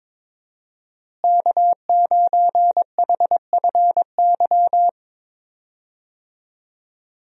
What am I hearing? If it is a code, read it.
Morse code, heard as a keyed tone, about 22 wpm, 700 Hz. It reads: K9HFY
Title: USA Call Signs - 5 Characters - Repeated 2x - 22wpm